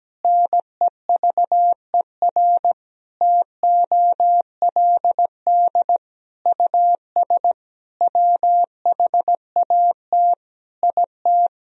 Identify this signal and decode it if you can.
Morse code, heard as a keyed tone, about 17 wpm, 700 Hz. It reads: NEVER TOLD US WHAT IT
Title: Sentences from Top 400 Words and Abbreviations 17wpm